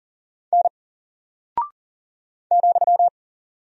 Morse code, heard as a keyed tone, about 40 wpm, 700 Hz. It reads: N ,
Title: Single Character (Letter:Number:Punctuation:Prosign) Alt 40wpm